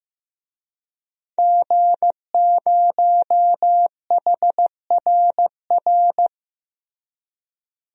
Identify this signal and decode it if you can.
Morse code, heard as a keyed tone, about 15 wpm, 700 Hz. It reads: G0HRR
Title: DXCC-100 Call Signs - 5 Characters - Repeated 3x - 15wpm